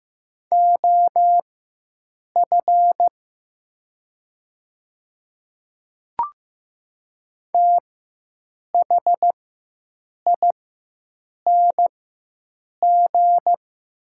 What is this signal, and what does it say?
Morse code, heard as a keyed tone, about 15 wpm, spaced as for 7 wpm, 700 Hz. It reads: OF THING
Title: 4x Character Spacing - Top 200 Words - 15wpm